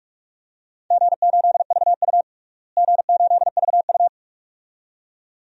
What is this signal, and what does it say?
Morse code, heard as a keyed tone, about 46 wpm, 700 Hz. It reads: G8VU G8VU
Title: DXCC-100 Call Signs - 4 Characters - Repeated 2x - 45wpm